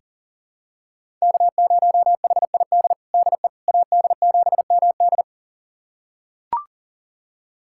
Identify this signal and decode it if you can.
Morse code, heard as a keyed tone, about 40 wpm, 700 Hz. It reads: K0HID DE AD7MD
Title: USA Callsign DE Callsign - 5 Characters - 40wpm